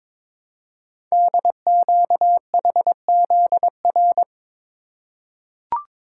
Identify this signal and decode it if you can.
Morse code, heard as a keyed tone, about 22 wpm, 690 Hz. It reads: DQHZR
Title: Code Groups - 5 Letter:Numbers - 22wpm